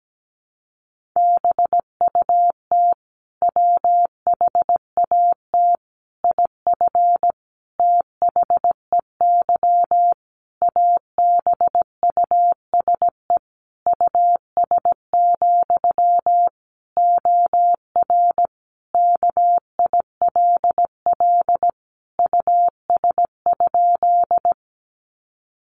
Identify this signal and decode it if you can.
Morse code, heard as a keyed tone, about 17 wpm, 700 Hz. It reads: BUT WHAT IF THEY ABUSE US, OR KILL US?
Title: Sentences from Top 1400 Words 17wpm